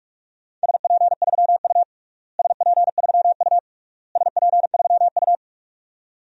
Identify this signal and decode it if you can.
Morse code, heard as a keyed tone, about 42 wpm, 700 Hz. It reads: SP3U SP3U SP3U
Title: DXCC-100 Call Signs - 4 Characters - Repeated 3x - 45wpm